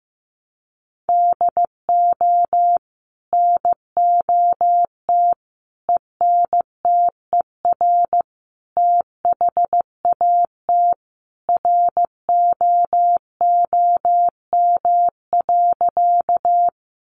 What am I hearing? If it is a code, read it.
Morse code, heard as a keyed tone, about 15 wpm, 700 Hz. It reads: DO NOT ENTER THAT ROOM.
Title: Sentences from Top 1300 Words 15wpm